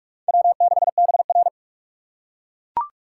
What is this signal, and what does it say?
Morse code, heard as a keyed tone, about 44 wpm, 690 Hz. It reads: W6BR
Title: USA Call Signs - 4 Characters - 8x Character Spacing - 45wpm